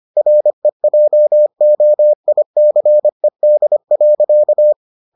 Morse code, heard as a keyed tone, about 25 wpm, 600 Hz. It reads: REJOICED.